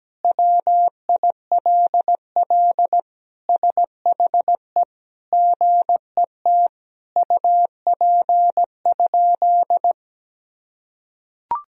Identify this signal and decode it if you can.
Morse code, heard as a keyed tone, about 17 wpm, 700 Hz. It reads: WILL SHE GET UP?
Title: Sentences from Top 1000 Words - Review Part 1 - 17wpm